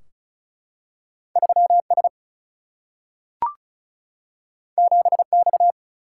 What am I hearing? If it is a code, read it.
Morse code, heard as a keyed tone, about 36 wpm, 700 Hz. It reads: 3S 7X
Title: Single-Letter Single-Number 35wpm